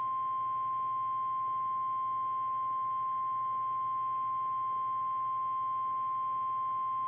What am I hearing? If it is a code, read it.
Emergency Alert System (SAME) tones (the 1050 Hz attention tone). It is heard from 0.3 to 6.7 s.
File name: eas / KKPRKXPN / KKPRKXPN